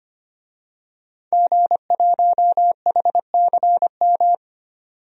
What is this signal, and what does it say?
Morse code, heard as a keyed tone, about 25 wpm, 700 Hz. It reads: G1HCM
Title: DXCC-100 Call Signs - 5 Characters - 25wpm